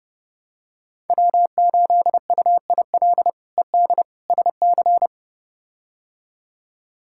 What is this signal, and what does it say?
Morse code, heard as a keyed tone, about 30 wpm, 700 Hz. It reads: W8UIL ED SC
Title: K1USN SST QSO Practice - 30wpm